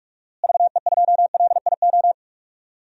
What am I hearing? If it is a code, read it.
Morse code, heard as a keyed tone, about 41 wpm, 700 Hz. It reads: VE2LIO
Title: DXCC-100 Call Signs - 6 Characters - Repeated 2x - 45wpm